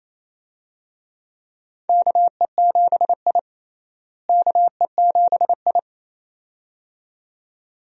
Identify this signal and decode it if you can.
Morse code, heard as a keyed tone, about 28 wpm, 700 Hz. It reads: KE7I KE7I
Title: USA Call Signs - 4 Characters - Repeated 2x - 28wpm